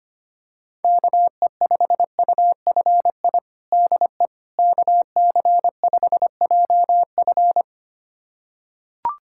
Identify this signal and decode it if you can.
Morse code, heard as a keyed tone, about 25 wpm, 700 Hz. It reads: KE5UFI DE KC5JF
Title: USA Callsign DE Callsign 25wpm